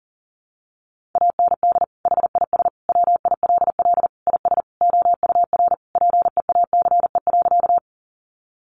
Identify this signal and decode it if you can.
Morse code, heard as a keyed tone, about 40 wpm, 700 Hz. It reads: AND HIS WILL IS OUR PEACE.
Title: Sentences from Top 1500 Words 40wpm